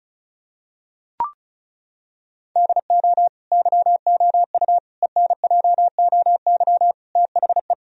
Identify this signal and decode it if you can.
Morse code, heard as a keyed tone, about 36 wpm, 700 Hz. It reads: DO YOU ENJOY THE
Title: Sentences from Top 1000 Words - Review Part 4 - 35wpm